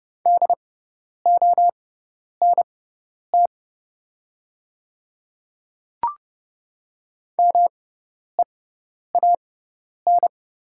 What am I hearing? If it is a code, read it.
Morse code, heard as a keyed tone, about 30 wpm, spaced as for 10 wpm, 700 Hz. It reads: DONT MEAN
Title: 6x Character Spacing - Top 300 Words - 30wpm